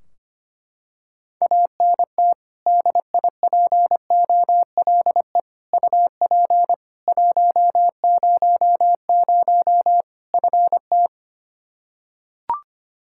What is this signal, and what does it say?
Morse code, heard as a keyed tone, about 25 wpm, 700 Hz. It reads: ANT DIPOLE UP 100 FT
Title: Practice QSO - Exchange 2 - Info - 25wpm